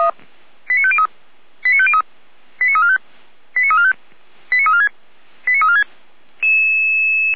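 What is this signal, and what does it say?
Five-tone selective call (ZVEI), heard as quick runs of five tones, tones of about 70 ms. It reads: 89503 89503 89345 89345 89346 89346
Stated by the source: ZVEI five-tone selective call received on an SDR, from a Feuerwehr Orlamünde (fire brigade) audio server.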